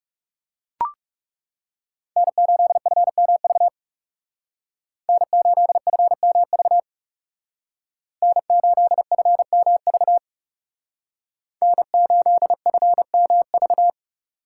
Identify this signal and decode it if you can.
Morse code, heard as a keyed tone, about 35 wpm, 700 Hz. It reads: N8FMV N8FMV N8FMV N8FMV
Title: Speed Racing - USA 1x3 Call Signs 30wpm